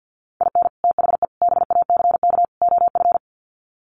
Morse code, heard as a keyed tone, about 48 wpm, 700 Hz. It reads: IN THE BACK OF
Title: Sets of 4 Words 50wpm